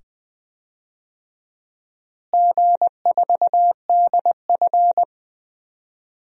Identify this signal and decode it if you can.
Morse code, heard as a keyed tone, about 20 wpm, 700 Hz. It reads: G4DF
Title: DXCC-100 Call Signs - 4 Characters - Repeated 2x - 20wpm